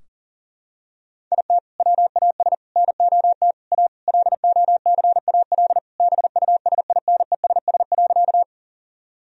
Morse code, heard as a keyed tone, about 37 wpm, 700 Hz. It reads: IT WAS NOT A LOCAL BUSINESS.
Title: Sentences from Top 1000 Words - Review Part 3 - 40wpm